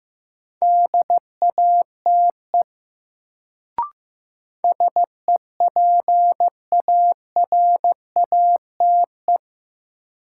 Morse code, heard as a keyed tone, about 15 wpm, 700 Hz. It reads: DATE SEPARATE
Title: Sentences from Top 1000 Words - Review Part 7 - New Word Drill - 15wpm